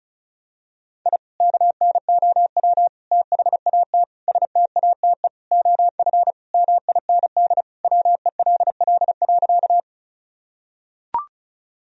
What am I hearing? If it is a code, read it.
Morse code, heard as a keyed tone, about 36 wpm, 700 Hz. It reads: I KNOW THAT STATE OF MIND WELL.
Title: Sentences from Top 1000 Words - Review Part 2 - 35wpm